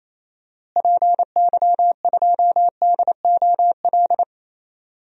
Morse code, heard as a keyed tone, about 27 wpm, 700 Hz. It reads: PY2DOL
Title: DXCC-100 Call Signs - 6 Characters - Repeated 2x - 28wpm